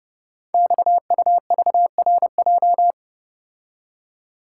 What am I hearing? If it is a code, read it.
Morse code, heard as a keyed tone, about 29 wpm, 700 Hz. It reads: XUVRJ
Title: Code Groups - 5 Letters - 30wpm